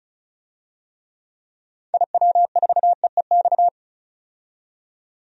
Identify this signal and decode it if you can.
Morse code, heard as a keyed tone, about 36 wpm, 700 Hz. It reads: IW4EEX
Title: DXCC-100 Call Signs - 6 Characters - 35wpm